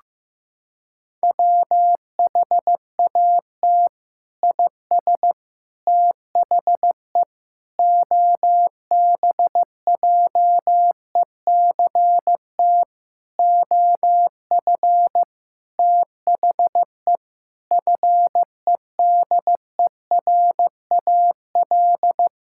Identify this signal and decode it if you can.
Morse code, heard as a keyed tone, about 15 wpm, 700 Hz. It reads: WHAT IS THE OBJECT OF THE FEDERAL
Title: Sentences from Top 1000 Words - Review Part 5 - 15wpm